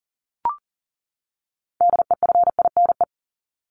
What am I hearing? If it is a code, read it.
Morse code, heard as a keyed tone, about 40 wpm, 700 Hz. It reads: DEFINE
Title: ED Suffix - Top 100 Words with Base Words 40wpm